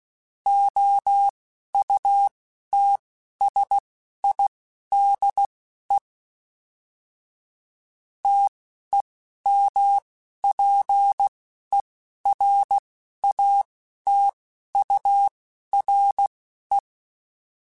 Morse code, heard as a keyed tone, about 16 wpm, spaced as for 11 wpm, 800 Hz. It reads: OUTSIDE TEMPERATURE
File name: Weather Sets / Weather Sets-mp3s / Weather 4 24 @10